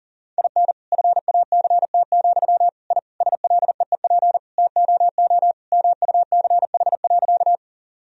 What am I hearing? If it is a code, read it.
Morse code, heard as a keyed tone, about 40 wpm, 700 Hz. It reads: IN FACT, I SLEEP TOO MUCH.
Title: Sentences from Top 1000 Words - Review Part 3 - 40wpm